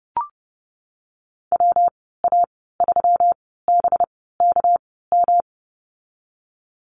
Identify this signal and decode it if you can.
Morse code, heard as a keyed tone, about 30 wpm, spaced as for 17 wpm, 700 Hz. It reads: WA3BKM
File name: Lesson 37g - Letter x - Call Signs 30wpm